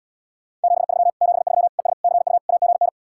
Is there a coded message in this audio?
Yes: Morse code from 0.6 to 2.9 s, heard as a keyed tone, about 78 wpm, 700 Hz. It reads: 73 73 ES CU AGN